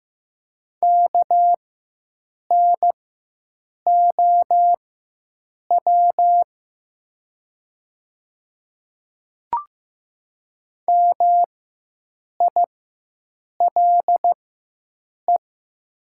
Morse code, heard as a keyed tone, about 15 wpm, spaced as for 7 wpm, 700 Hz. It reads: KNOW MILE
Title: 4x Character Spacing - Top 500 Words - 15wpm